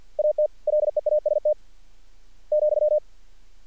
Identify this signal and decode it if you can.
Morse code, heard as a keyed tone, about 50 wpm, 600 Hz. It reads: AT LEAST ,
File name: archived / CRSnet